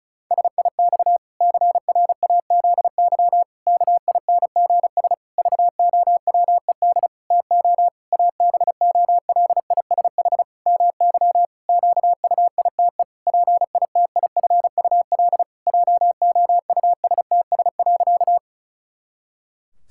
Morse code, heard as a keyed tone, about 34 wpm, 700 Hz. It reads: SIX CRAZY KINGS VOWED TO ABOLISH MY QUITE PITIFUL JOUSTS.
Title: Pangram Challenge v2 - 35wpm